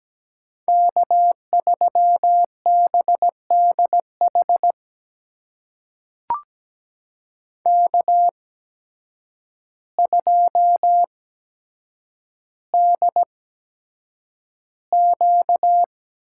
Morse code, heard as a keyed tone, about 17 wpm, spaced as for 8 wpm, 700 Hz. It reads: K3BDH K2DQ
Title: USA Call Signs - 8x Character Spacing - 17wpm